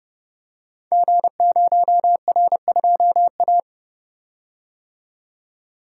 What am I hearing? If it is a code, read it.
Morse code, heard as a keyed tone, about 30 wpm, 700 Hz. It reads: G0R2A
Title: Code Groups - 5 Letter:Numbers - 30wpm